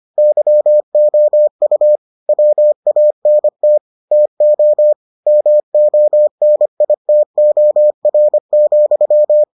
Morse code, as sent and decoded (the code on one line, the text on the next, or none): -.-- --- ..- / .-- .- -. - / - --- / -- --- -. .. - --- .-. --..--
YOU WANT TO MONITOR,